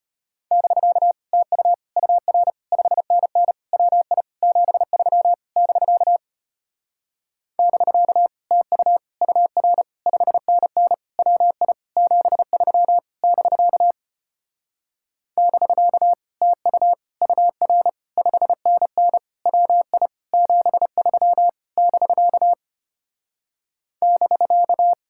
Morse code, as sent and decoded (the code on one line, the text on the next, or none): -...-.- / - ..- / ..- .-. / ..... -. -. / .-- .. / --... ...-- / -...-.- / -...-.- / - ..- / ..- .-. / ..... -. -. / .-- .. / --... ...-- / -...-.- / -...-.- / - ..- / ..- .-. / ..... -. -. / .-- .. / --... ...-- / -...-.- / -...-.-
<BK> TU UR 5NN WI 73 <BK> <BK> TU UR 5NN WI 73 <BK> <BK> TU UR 5NN WI 73 <BK> <BK>